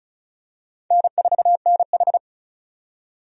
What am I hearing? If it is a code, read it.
Morse code, heard as a keyed tone, about 36 wpm, 700 Hz. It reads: N4NH
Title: USA Call Signs - 4 Characters - 35wpm